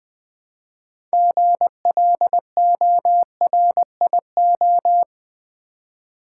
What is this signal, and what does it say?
Morse code, heard as a keyed tone, about 20 wpm, 700 Hz. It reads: GLORIO
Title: Build A Word - 3 Letters At A Time - 8-Letter Words - 20wpm